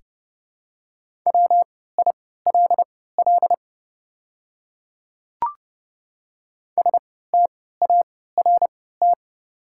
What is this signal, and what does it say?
Morse code, heard as a keyed tone, about 30 wpm, spaced as for 17 wpm, 690 Hz. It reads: WILL START
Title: 3x Character Spacing - Top 300 Words - 30wpm